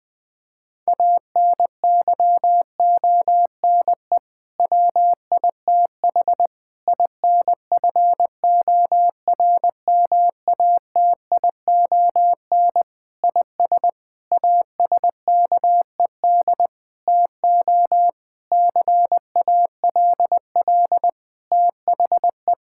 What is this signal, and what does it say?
Morse code, heard as a keyed tone, about 20 wpm, 700 Hz. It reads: ANYONE WITH INFORMATION IS ASKED TO CALL THE
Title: Sentences with 6-Word Phrases - Spoken Beginning - 20wpm